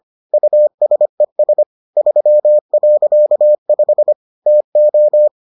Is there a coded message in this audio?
Yes: Morse code from 0.3 to 5.3 s, heard as a keyed tone, about 25 wpm, 600 Hz. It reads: USES 3.5 TO